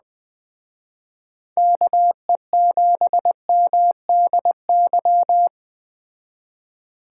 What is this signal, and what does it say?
Morse code, heard as a keyed tone, about 20 wpm, 700 Hz. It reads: KE7MDY
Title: Speed Racing - USA 2x3 Call Signs 17wpm